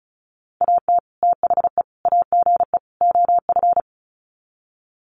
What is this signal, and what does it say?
Morse code, heard as a keyed tone, about 34 wpm, 700 Hz. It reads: AT THE AGE OF
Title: Sets of 4 Words 35wpm